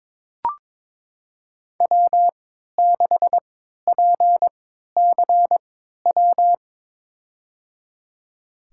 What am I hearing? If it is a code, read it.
Morse code, heard as a keyed tone, about 22 wpm, spaced as for 13 wpm, 700 Hz. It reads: W6PCW
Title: USA Call Signs - 3x Character Spacing - 22wpm